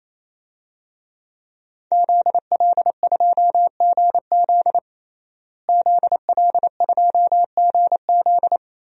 Morse code, heard as a keyed tone, about 28 wpm, 700 Hz. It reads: ZL2GZ ZL2GZ
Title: DXCC-100 Call Signs - 5 Characters - Repeated 3x - 28wpm